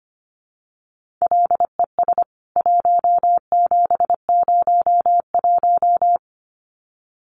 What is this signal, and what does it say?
Morse code, heard as a keyed tone, about 25 wpm, 700 Hz. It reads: LES 1701
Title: CWT Exchange 25wpm